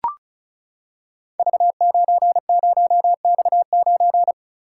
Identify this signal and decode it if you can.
Morse code, heard as a keyed tone, about 35 wpm, 700 Hz. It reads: V90X9
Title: Code Groups - 5 Letter:Numbers - 35wpm